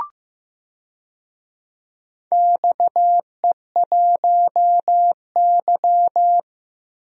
Morse code, heard as a keyed tone, about 15 wpm, 700 Hz. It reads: XE1Y
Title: DXCC-100 Call Signs - 4 Characters - 15wpm